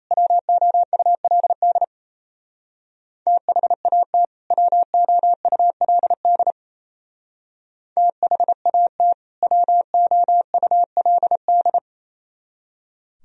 Morse code, heard as a keyed tone, about 35 wpm, 700 Hz. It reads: WOULD THAT WOULD THAT WOULD
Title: Speed-Racing - Sets of 2 Words 28wpm